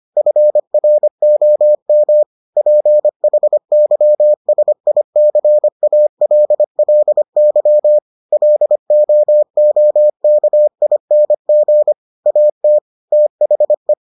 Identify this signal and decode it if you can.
Morse code, heard as a keyed tone, about 25 wpm, 600 Hz. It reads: FROM PHYSICALLY LOOKING AT THE